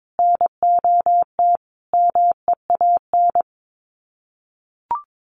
Morse code, heard as a keyed tone, about 22 wpm, 700 Hz. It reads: NOT MEAN
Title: Sets of 4 Words 22wpm